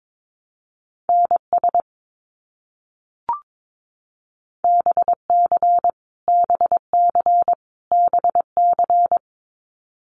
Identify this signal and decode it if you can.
Morse code, heard as a keyed tone, about 22 wpm, 700 Hz. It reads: NS BC BC BC
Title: Canadian Provinces and Territories - Repeated 3x - 22wpm